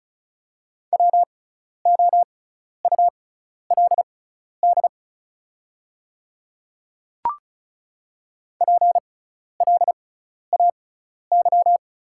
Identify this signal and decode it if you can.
Morse code, heard as a keyed tone, about 36 wpm, spaced as for 12 wpm, 690 Hz. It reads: WOULD PLAY
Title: 6x Character Spacing - Top 200 Words - 35wpm